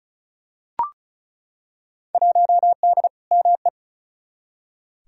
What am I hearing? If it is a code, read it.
Morse code, heard as a keyed tone, about 36 wpm, 700 Hz. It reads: 1D ME
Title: Field Day Exchanges - 35wpm